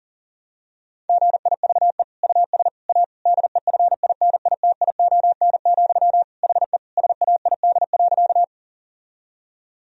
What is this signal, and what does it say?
Morse code, heard as a keyed tone, about 40 wpm, 700 Hz. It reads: GIVE US A DEFINITION, HE SAID.
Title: Sentences from Top 1000 Words - Review Part 4 - 40wpm